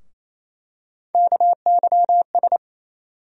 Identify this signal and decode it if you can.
Morse code, heard as a keyed tone, about 28 wpm, 700 Hz. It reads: KYS
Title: USA Call Sign Postfixes - 3 Letters 28wpm